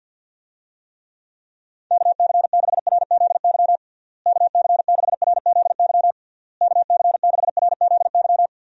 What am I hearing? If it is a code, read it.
Morse code, heard as a keyed tone, about 50 wpm, 700 Hz. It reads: KC6RZY KC6RZY KC6RZY
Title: USA Call Signs - 6 Characters - Repeated 3x - 50wpm